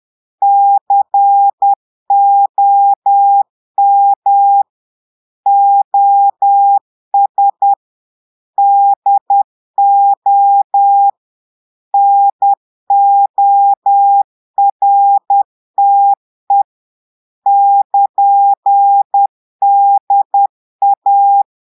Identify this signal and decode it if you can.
Morse code, heard as a keyed tone, about 10 wpm, 800 Hz. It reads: COM OS DO NORTE <KN>DA
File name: Portugais - texte en clair 1 - vitesse - 10 mots minute